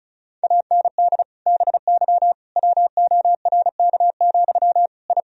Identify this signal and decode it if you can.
Morse code, heard as a keyed tone, about 34 wpm, 700 Hz. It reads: AND BY WORK, I
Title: Sentences from Top 1000 Words - Review Part 4 - 35wpm